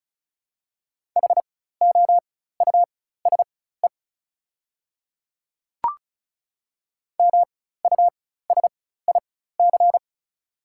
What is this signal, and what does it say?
Morse code, heard as a keyed tone, about 36 wpm, spaced as for 16 wpm, 700 Hz. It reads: HOUSE MUSIC
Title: 4x Character Spacing - Top 300 Words - 35wpm